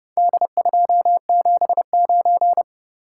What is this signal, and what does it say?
Morse code, heard as a keyed tone, about 30 wpm, 700 Hz. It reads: D279